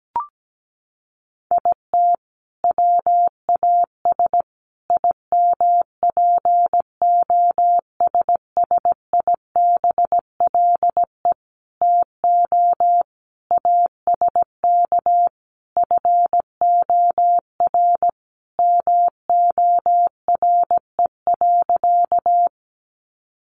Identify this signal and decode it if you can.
Morse code, heard as a keyed tone, about 17 wpm, 700 Hz. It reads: IT WAS IMPOSSIBLE TO ASK FOR MORE.
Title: Sentences from Top 1100 Words 17wpm